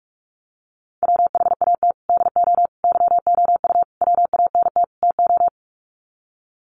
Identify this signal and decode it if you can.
Morse code, heard as a keyed tone, about 44 wpm, 700 Hz. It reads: WHAT DO YOU WANT TO
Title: Sets of 5 Words 45wpm